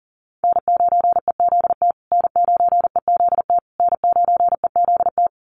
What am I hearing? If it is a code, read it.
Morse code, heard as a keyed tone, about 40 wpm, 700 Hz. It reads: N9EZT N9EZT N9EZT
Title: USA Call Signs Repeated 3x 40wpm